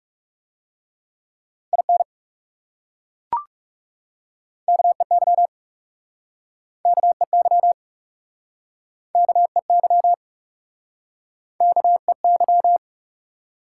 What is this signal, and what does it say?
Morse code, heard as a keyed tone, about 38 wpm, 700 Hz. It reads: IN KEY KEY KEY KEY